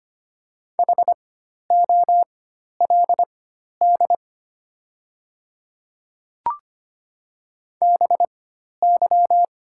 Morse code, heard as a keyed tone, about 25 wpm, spaced as for 12 wpm, 700 Hz. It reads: HOLD BY
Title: 4x Character Spacing - Top 500 Words - 25wpm